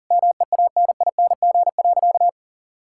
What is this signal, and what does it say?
Morse code, heard as a keyed tone, about 40 wpm, 700 Hz. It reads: MEANING.
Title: Sentences from Top 1000 Words - Review Part 6 - 40wpm